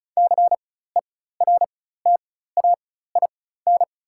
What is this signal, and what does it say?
Morse code, heard as a keyed tone, about 34 wpm, spaced as for 16 wpm, 700 Hz. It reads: CERTAIN
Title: 4x Character Spacing - Top 500 Words - 35wpm